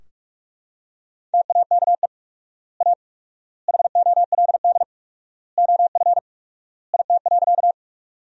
Morse code, heard as a keyed tone, about 41 wpm, 700 Hz. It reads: TAKE A HOLD OF IT.
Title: 4x Word-Spacing - Sentences from Top 400 Words - 45wpm